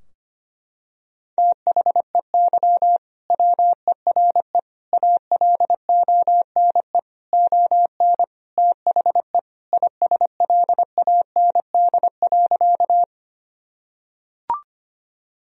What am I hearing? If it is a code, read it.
Morse code, heard as a keyed tone, about 25 wpm, 700 Hz. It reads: THEY WERE ALONE ON THE ISLAND.
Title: Sentences from Top 1000 Words - Review Part 3 - 25wpm